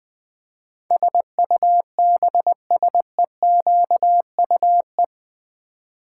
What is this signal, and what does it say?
Morse code, heard as a keyed tone, about 20 wpm, 700 Hz. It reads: SUBSEQUE
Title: Build A Word - 4 Letters At A Time - 12-Letter Words - 20wpm